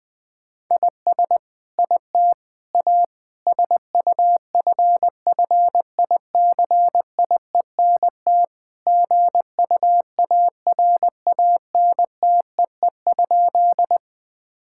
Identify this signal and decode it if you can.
Morse code, heard as a keyed tone, about 21 wpm, 700 Hz. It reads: IS IT A SUFFICIENT GUARANTEE?
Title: Sentences from Top 1600 Words 20wpm